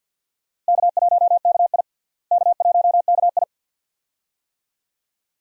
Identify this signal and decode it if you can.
Morse code, heard as a keyed tone, about 52 wpm, 700 Hz. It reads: K1KI K1KI
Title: USA Call Signs - 4 Characters - Repeated 2x - 50wpm